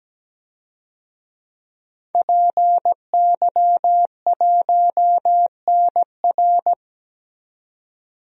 Morse code, heard as a keyed tone, about 17 wpm, 700 Hz. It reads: PY1NR
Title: DXCC-100 Call Signs - 5 Characters - 17wpm